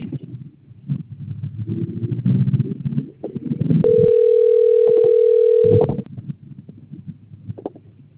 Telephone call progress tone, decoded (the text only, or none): ringback tone